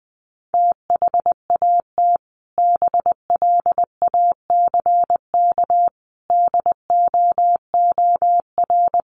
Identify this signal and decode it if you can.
Morse code, heard as a keyed tone, about 19 wpm, 700 Hz. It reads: THAT BLACK DOOR
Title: Sentences from Top 1400 Words 20wpm